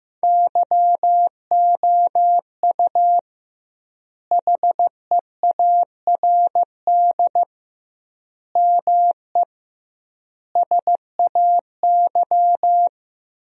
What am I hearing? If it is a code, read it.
Morse code, heard as a keyed tone, about 15 wpm, 700 Hz. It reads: YOU HEARD ME SAY
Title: 2x Word-Spacing - Sentences from Top 500 Words - 15wpm